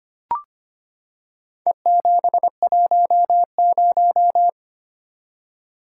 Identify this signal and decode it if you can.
Morse code, heard as a keyed tone, about 25 wpm, 700 Hz. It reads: E710
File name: practice / 25wpm / Four Letter-Number Combinations - 25wpm